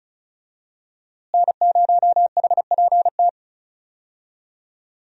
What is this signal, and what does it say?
Morse code, heard as a keyed tone, about 36 wpm, 700 Hz. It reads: N0HPT
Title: USA Call Signs - 5 Characters - Repeated 3x - 35wpm